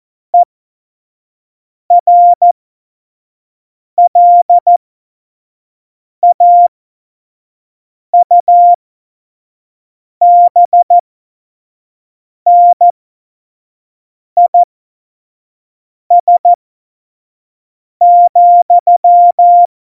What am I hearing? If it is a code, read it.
Morse code, heard as a keyed tone, about 14 wpm, spaced as for 5 wpm, 700 Hz. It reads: ERLAUBNIS,